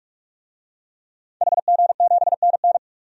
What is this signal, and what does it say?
Morse code, heard as a keyed tone, about 41 wpm, 700 Hz. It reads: HG7NN
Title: DXCC-100 Call Signs - 5 Characters - 45wpm